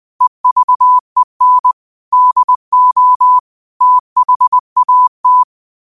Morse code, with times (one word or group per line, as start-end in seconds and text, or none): 0.2-1.7 EVEN
2.1-3.4 DO
3.8-5.4 THAT